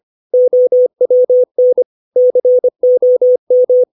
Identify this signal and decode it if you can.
Morse code, heard as a keyed tone, about 25 wpm, 500 Hz. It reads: OWN COM
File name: 632 words / 632 Words 25WPM